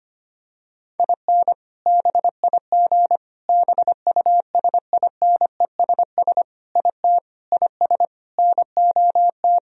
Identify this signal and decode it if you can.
Morse code, heard as a keyed tone, about 25 wpm, 700 Hz. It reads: IN BIG BUSINESS IT IS NOT
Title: Sentences from Top 1000 Words - Review Part 1 - 25wpm